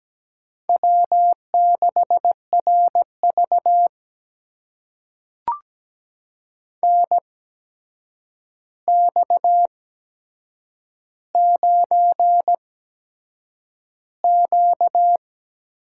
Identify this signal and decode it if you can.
Morse code, heard as a keyed tone, about 17 wpm, spaced as for 7 wpm, 700 Hz. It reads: W6RV NX9Q
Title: USA Call Signs - 4 Characters - 8x Character Spacing - 17wpm